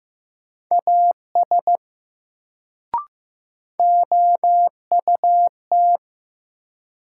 Morse code, heard as a keyed tone, about 15 wpm, 700 Hz. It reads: AS OUT
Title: Sets of 2 Words - New Word Drill - 15wpm